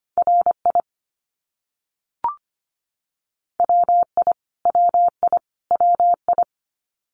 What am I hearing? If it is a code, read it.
Morse code, heard as a keyed tone, about 25 wpm, 700 Hz. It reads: RI WI WI WI
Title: US States Repeated 3x 25wpm